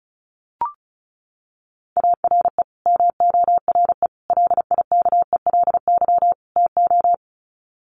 Morse code, heard as a keyed tone, about 34 wpm, 700 Hz. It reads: ARE MORE LIKELY TO
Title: Sets of 4 Words 35wpm